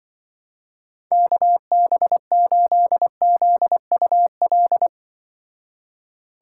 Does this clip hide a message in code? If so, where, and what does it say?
Yes: Morse code from 1.1 to 4.9 s, heard as a keyed tone, about 25 wpm, 700 Hz. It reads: KB8ZUL